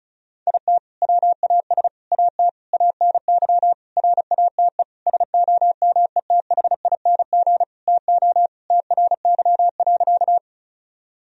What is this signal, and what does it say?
Morse code, heard as a keyed tone, about 36 wpm, 700 Hz. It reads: IT WAS AT ANY RATE SOMETHING TO TRY.
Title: Sentences from Top 1000 Words - Review Part 4 - 35wpm